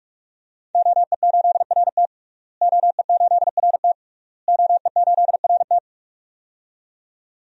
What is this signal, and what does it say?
Morse code, heard as a keyed tone, about 41 wpm, 700 Hz. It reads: OE8RT OE8RT OE8RT
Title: DXCC-100 Call Signs - 5 Characters - Repeated 3x - 45wpm